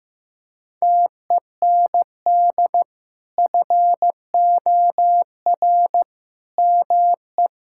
Morse code, heard as a keyed tone, about 15 wpm, 700 Hz. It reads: TEND FOR ME
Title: Sentences from Top 1000 Words - Review Part 5 - 15wpm